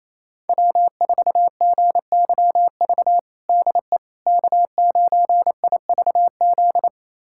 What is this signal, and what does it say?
Morse code, heard as a keyed tone, about 28 wpm, 700 Hz. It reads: W4GYV DE K9IVZ
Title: USA Callsign DE Callsign - 5 Characters - 28wpm